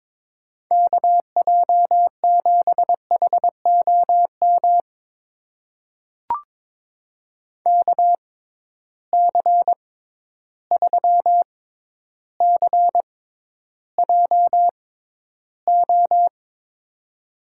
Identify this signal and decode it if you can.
Morse code, heard as a keyed tone, about 22 wpm, spaced as for 11 wpm, 700 Hz. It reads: KJ7HOM KC3CJO
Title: USA Call Signs - 6 Characters - 6x Character Spacing - 22wpm